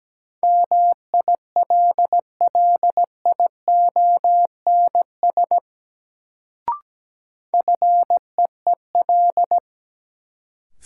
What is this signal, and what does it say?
Morse code, heard as a keyed tone, about 17 wpm, 700 Hz. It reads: MILLIONS FEEL
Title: Sets of 5 Words - New Word Drill - 17wpm